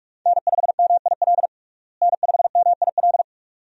Morse code, heard as a keyed tone, about 46 wpm, 700 Hz. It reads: N5MIL N5MIL
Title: USA Call Signs - 5 Characters - Repeated 3x - 45wpm